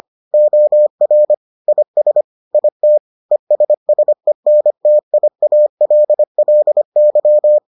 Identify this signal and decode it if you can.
Morse code, heard as a keyed tone, about 25 wpm, 600 Hz. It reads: OR IS IT ESSENTIALLY